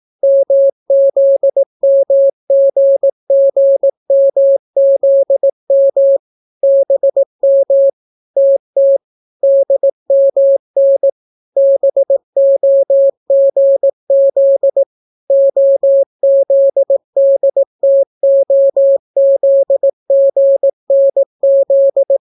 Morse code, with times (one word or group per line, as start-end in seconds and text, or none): 0.2-6.2 MZMGGMZM
6.6-7.9 BM
8.4-9.0 TT
9.4-11.1 DMN
11.6-14.8 BOGZ
15.3-22.2 OZDTOZGNZ